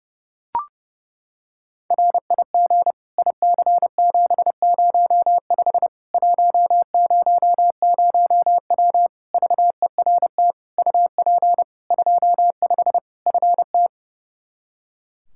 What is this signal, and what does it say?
Morse code, heard as a keyed tone, about 32 wpm, 700 Hz. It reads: RIG IC705 100W VERT UP 25 FT
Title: K1USN SST QSO Practice 30wpm